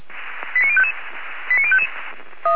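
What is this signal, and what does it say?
Five-tone selective call (ZVEI), heard as quick runs of five tones, tones of about 70 ms. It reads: 89055 89055
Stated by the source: ZVEI five-tone selective call 89055, received on SDR.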